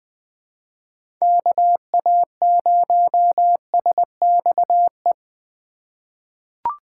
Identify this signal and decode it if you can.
Morse code, heard as a keyed tone, about 20 wpm, 700 Hz. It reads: KA0SXE
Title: USA Call Signs - 6 Characters - 3x Character Spacing - 20wpm